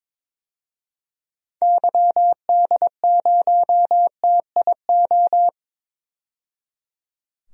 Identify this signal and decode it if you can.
Morse code, heard as a keyed tone, about 21 wpm, 700 Hz. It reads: YD0TIO
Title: DXCC-100 Call Signs - 6 Characters - 22wpm